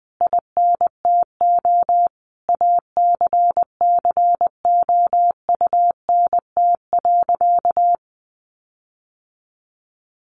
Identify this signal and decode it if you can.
Morse code, heard as a keyed tone, about 20 wpm, 700 Hz. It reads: INTO ACCOUNT.
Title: Sentences from Top 1300 Words 20wpm